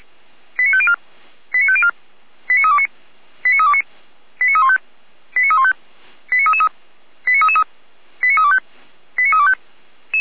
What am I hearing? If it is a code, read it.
Five-tone selective call (ZVEI), heard as quick runs of five tones, tones of about 80 ms. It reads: 89504 89504 89329 89329 89315 89315 89333 89325 89325
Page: ZVEI